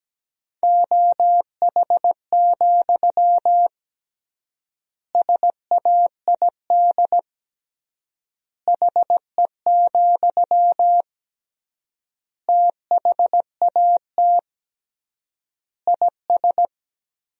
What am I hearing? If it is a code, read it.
Morse code, heard as a keyed tone, about 17 wpm, 700 Hz. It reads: OH, SAID HE, THAT IS
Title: 3x Word-Spacing - Sentences from Top 500 Words - 17wpm